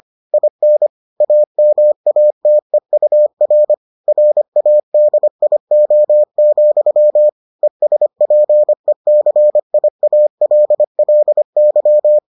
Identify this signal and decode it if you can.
Morse code, heard as a keyed tone, about 25 wpm, 600 Hz. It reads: IN AMATEUR RADIO, ESPECIALLY